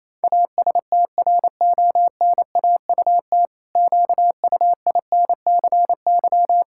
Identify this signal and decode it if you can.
Morse code, heard as a keyed tone, about 28 wpm, 700 Hz. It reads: ASTRONAUT QUINCY
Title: Pangram Challenge v2 - 28wpm